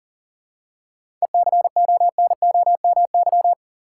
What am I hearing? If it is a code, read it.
Morse code, heard as a keyed tone, about 40 wpm, 700 Hz. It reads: ECONOMY
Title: Seven Letter Words - 40wpm